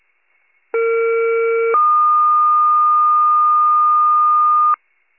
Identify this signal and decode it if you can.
Two-tone page, heard as a one-second tone, then a three-second tone, about 460 then 1220 Hz. It is heard from 0.7 to 4.7 s.
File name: pager / 21 RESCUE / wcalert